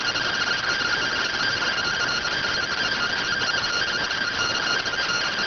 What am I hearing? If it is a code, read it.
Two-tone teleprinter signal, about 170 Hz shift.